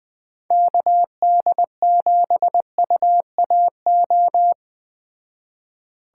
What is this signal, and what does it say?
Morse code, heard as a keyed tone, about 21 wpm, 700 Hz. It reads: KD7UAO